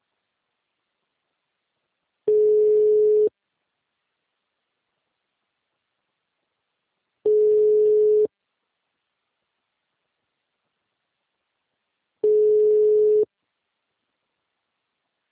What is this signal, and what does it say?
Ringback tone (Europe, 425 Hz).